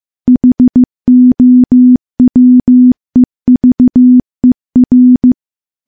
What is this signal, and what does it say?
Morse code, heard as a keyed tone, about 15 wpm, 250 Hz. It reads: HOWEVER
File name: morse / morse-training-15WPM